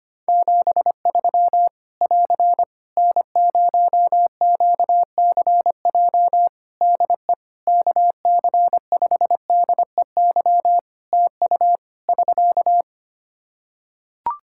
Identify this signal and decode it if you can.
Morse code, heard as a keyed tone, about 25 wpm, 700 Hz. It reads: 73 <AR> N0QCJ DE KC5DEY TU <SK>
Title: Practice QSO - Exchange 4 - Ending - 25wpm